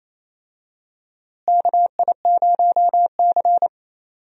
Morse code, heard as a keyed tone, about 28 wpm, 700 Hz. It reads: KI0C
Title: USA Call Signs - 4 Characters - Repeated 2x - 28wpm